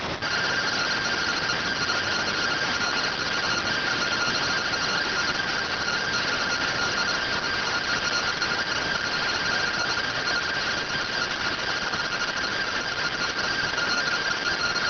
Two-tone teleprinter signal, about 170 Hz shift.